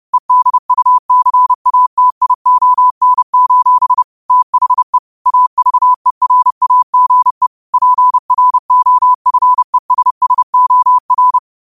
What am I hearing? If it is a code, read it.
Morse code, heard as a keyed tone, about 30 wpm, 1000 Hz. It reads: EDUCATION: THE AVERAGE PROFESSOR